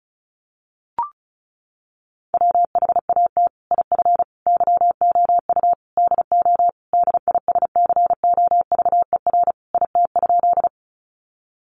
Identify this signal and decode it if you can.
Morse code, heard as a keyed tone, about 34 wpm, 700 Hz. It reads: WHAT IF YOU DO DISCOVER IT?
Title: Sentences from Top 1100 Words 35wpm